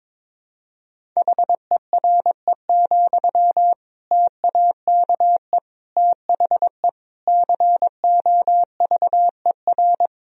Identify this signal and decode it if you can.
Morse code, heard as a keyed tone, about 22 wpm, 700 Hz. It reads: HERE, TAKE THE COVER
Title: Sentences from Top 1000 Words - Review Part 1 - 22wpm